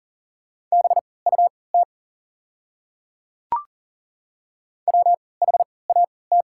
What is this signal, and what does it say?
Morse code, heard as a keyed tone, about 40 wpm, spaced as for 23 wpm, 700 Hz. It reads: BUT WHAT
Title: 3x Character Spacing - Top 100 Words - 40wpm